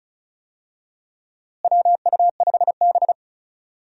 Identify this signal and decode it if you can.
Morse code, heard as a keyed tone, about 36 wpm, 700 Hz. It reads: WU5B
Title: USA Call Signs - 4 Characters - Repeated 2x - 35wpm